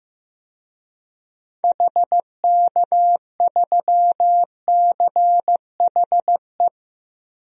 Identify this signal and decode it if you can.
Morse code, heard as a keyed tone, about 16 wpm, 700 Hz. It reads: HK3CHE
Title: DXCC-100 Call Signs - 6 Characters - 15wpm